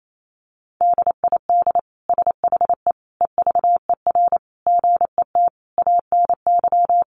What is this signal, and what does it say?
Morse code, heard as a keyed tone, about 28 wpm, 700 Hz. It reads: DID SHE EVER GET ANY
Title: Sentences from Top 1100 Words 28wpm